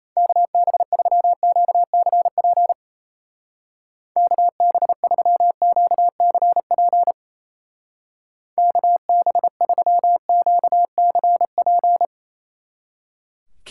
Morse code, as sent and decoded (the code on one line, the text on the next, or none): -.- -... ...-- --.- -.-. .--. / -.- -... ...-- --.- -.-. .--. / -.- -... ...-- --.- -.-. .--.
KB3QCP KB3QCP KB3QCP